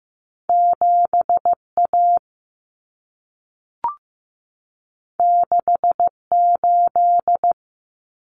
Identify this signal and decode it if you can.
Morse code, heard as a keyed tone, about 15 wpm, 700 Hz. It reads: 7 A 6 8
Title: Common Cut Numbers - 2-Digit 15wpm